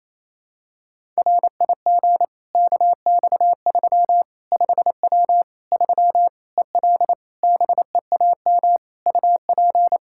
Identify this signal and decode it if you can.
Morse code, heard as a keyed tone, about 28 wpm, 700 Hz. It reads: RIG KX3 5W 3 EL BEAM UP
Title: K1USN SST QSO Practice 28wpm